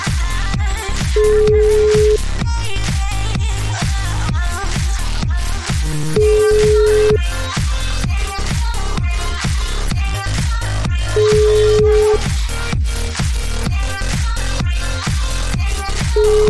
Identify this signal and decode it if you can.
Ringback tone (Europe, 425 Hz), with noise.